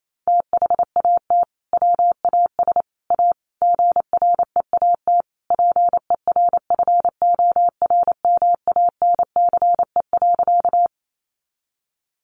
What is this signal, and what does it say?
Morse code, heard as a keyed tone, about 28 wpm, 700 Hz. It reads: THAT WAS A GREAT PERFORMANCE.
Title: Sentences from Top 1000 Words 28wpm